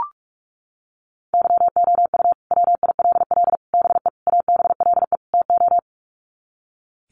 Morse code, heard as a keyed tone, about 44 wpm, 700 Hz. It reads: YOU WILL BE ABLE TO
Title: Sets of 5 Words Encore2 45wpm